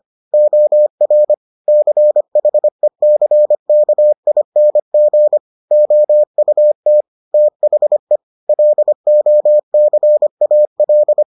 Morse code, heard as a keyed tone, about 25 wpm, 600 Hz. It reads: OR CHECKING OUT THE LOCAL